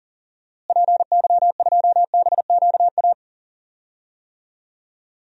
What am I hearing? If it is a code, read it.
Morse code, heard as a keyed tone, about 42 wpm, 700 Hz. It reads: PY2BQA
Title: DXCC-100 Call Signs - 6 Characters - Repeated 3x - 40wpm